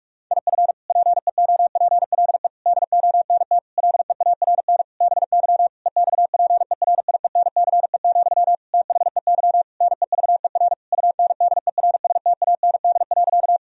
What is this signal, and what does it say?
Morse code, heard as a keyed tone, about 46 wpm, 700 Hz. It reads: IF PEOPLE DONT LEARN BY EXPERIENCE, THEY NEVER UNDERSTAND.
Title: Sentences from Top 1000 Words - Review Part 3 - 45wpm